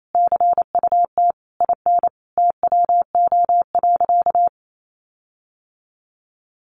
Morse code, heard as a keyed tone, about 28 wpm, 700 Hz. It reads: CUT IN TWO.
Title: Sentences from Top 1300 Words 28wpm